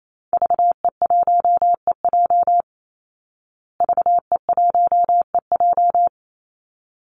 Morse code, heard as a keyed tone, about 28 wpm, 700 Hz. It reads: VE1EJ VE1EJ
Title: Call Signs Repeated 3x - 28wpm